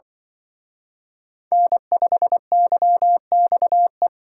Morse code, heard as a keyed tone, about 24 wpm, 700 Hz. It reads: N5YXE
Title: Speed Racing - USA 1x3 Call Signs 20wpm